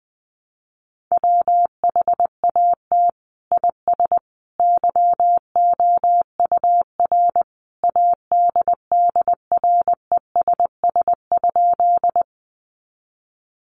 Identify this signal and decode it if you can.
Morse code, heard as a keyed tone, about 20 wpm, 700 Hz. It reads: WHAT IS YOUR ADDRESS?
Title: Sentences from Top 1000 Words 20wpm